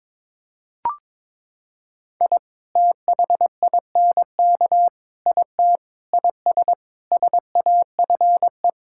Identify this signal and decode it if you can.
Morse code, heard as a keyed tone, about 22 wpm, 700 Hz. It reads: I THINK IT IS SAFE
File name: practice / 22wpm / Sentences with 8-Word Phrases - 22wpm